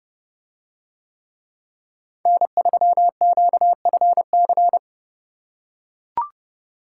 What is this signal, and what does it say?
Morse code, heard as a keyed tone, about 29 wpm, 700 Hz. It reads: N3QFC